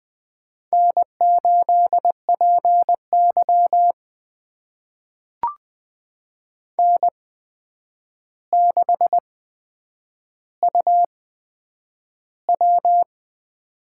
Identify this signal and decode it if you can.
Morse code, heard as a keyed tone, about 20 wpm, spaced as for 9 wpm, 700 Hz. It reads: N8PY N6UW
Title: USA Call Signs - 4 Characters - 8x Character Spacing - 20wpm